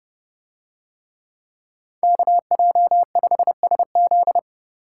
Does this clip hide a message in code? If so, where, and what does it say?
Yes: Morse code from 2.0 to 4.4 s, heard as a keyed tone, about 28 wpm, 700 Hz. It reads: KJ5SZ